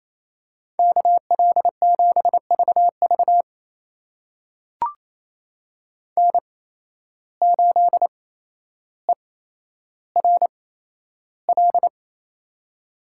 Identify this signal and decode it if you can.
Morse code, heard as a keyed tone, about 28 wpm, spaced as for 12 wpm, 700 Hz. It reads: KL7VV N8ERL
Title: USA Call Signs - 5 Characters - 8x Character Spacing - 28wpm